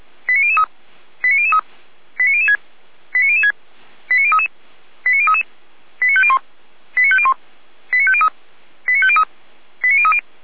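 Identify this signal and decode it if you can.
Five-tone selective call (ZVEI), heard as quick runs of five tones, tones of about 70 ms. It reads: 89003 89003 89006 89006 89033 89033 89501 89501 89503 89503 89030
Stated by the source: Five-tone ZVEI selective call received on an SDR.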